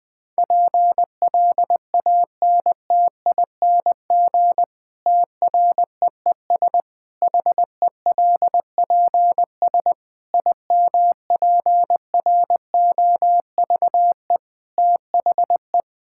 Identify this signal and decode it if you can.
Morse code, heard as a keyed tone, about 20 wpm, 700 Hz. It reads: PLANTING TREES HELPS IMPROVE THE
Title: Difficult Sentences with Hints - 20wpm